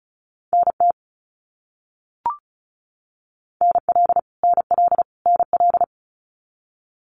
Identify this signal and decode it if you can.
Morse code, heard as a keyed tone, about 34 wpm, 700 Hz. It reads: NT NL NL NL
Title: Canadian Provinces and Territories - Repeated 3x - 35wpm